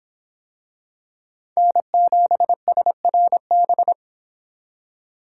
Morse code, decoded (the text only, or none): N7SRB